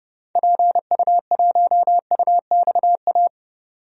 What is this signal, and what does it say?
Morse code, heard as a keyed tone, about 30 wpm, 700 Hz. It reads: PU1UXA
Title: DXCC-100 Call Signs - 6 Characters - 30wpm